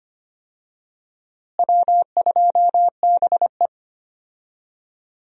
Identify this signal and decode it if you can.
Morse code, heard as a keyed tone, about 25 wpm, 700 Hz. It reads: W2BE
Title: USA Call Signs - 4 Characters - Repeated 2x - 25wpm